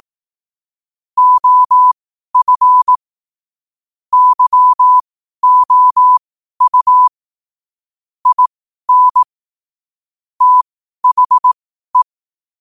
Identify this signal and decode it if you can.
Morse code, heard as a keyed tone, about 18 wpm, spaced as for 12 wpm, 1000 Hz. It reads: OF YOU IN THE
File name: qotdfiles / 12wpm